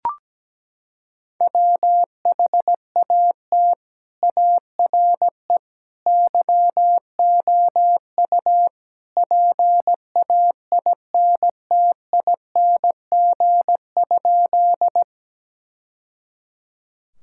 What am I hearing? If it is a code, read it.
Morse code, heard as a keyed tone, about 17 wpm, 700 Hz. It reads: WHAT ARE YOU PAINTING?
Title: Sentences from Top 1000 Words - Review Part 6 - 17wpm